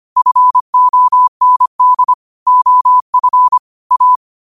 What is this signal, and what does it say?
Morse code, heard as a keyed tone, about 25 wpm, 1000 Hz. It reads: FOND OF A
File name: qotdfiles / 25wpm